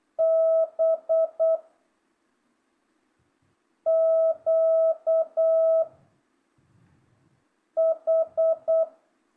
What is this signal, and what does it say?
Morse code, heard as a keyed tone, about 8 wpm, 660 Hz. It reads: B Q H
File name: jh8cbh / 2025cw